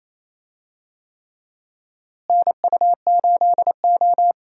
Morse code, heard as a keyed tone, about 28 wpm, 700 Hz. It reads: NU8O
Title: USA Call Signs - 4 Characters - Repeated 3x - 28wpm